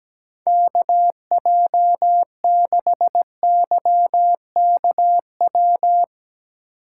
Morse code, heard as a keyed tone, about 17 wpm, 700 Hz. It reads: KJ6YKW
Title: USA Call Signs - 2x Character Spacing - 17wpm